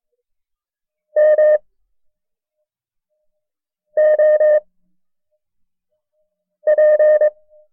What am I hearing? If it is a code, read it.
Morse code, heard as a keyed tone, about 22 wpm, spaced as for 4 wpm, 610 Hz. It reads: MOP
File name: FreeFiles / CW-practice / KB6MT-tape1-side2